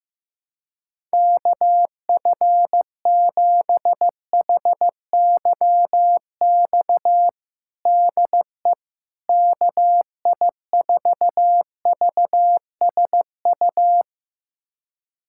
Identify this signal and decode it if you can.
Morse code, heard as a keyed tone, about 16 wpm, 700 Hz. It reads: KF7HYX DE KI4VSU
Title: USA Callsign DE Callsign - 6 Characters - 15wpm